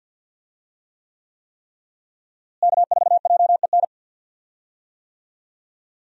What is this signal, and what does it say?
Morse code, heard as a keyed tone, about 46 wpm, 700 Hz. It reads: K4JEN
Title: USA Call Signs - 5 Characters - Repeated 3x - 50wpm